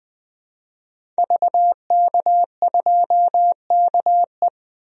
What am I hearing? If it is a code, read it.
Morse code, heard as a keyed tone, about 20 wpm, 700 Hz. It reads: VK2KE
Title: DXCC-100 Call Signs - 5 Characters - 20wpm